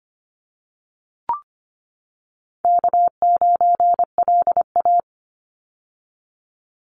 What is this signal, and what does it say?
Morse code, heard as a keyed tone, about 25 wpm, 700 Hz. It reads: K9LA
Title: Call Signs - 25wpm